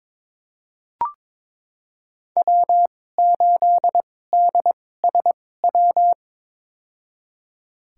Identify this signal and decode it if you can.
Morse code, heard as a keyed tone, about 22 wpm, spaced as for 16 wpm, 700 Hz. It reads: W8DSW
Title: USA Call Signs - 2x Character Spacing - 22wpm